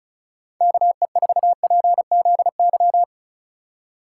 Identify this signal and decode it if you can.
Morse code, heard as a keyed tone, about 44 wpm, 700 Hz. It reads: KE4PZY